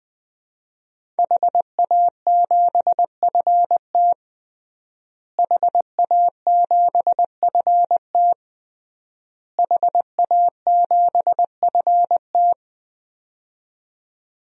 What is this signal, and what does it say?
Morse code, heard as a keyed tone, about 21 wpm, 700 Hz. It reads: HA7FT HA7FT HA7FT
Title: DXCC-100 Call Signs - 5 Characters - Repeated 3x - 20wpm